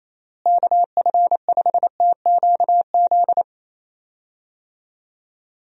Morse code, heard as a keyed tone, about 28 wpm, 700 Hz. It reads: KF5TQZ
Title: USA Call Signs - 6 Characters - Repeated 2x - 28wpm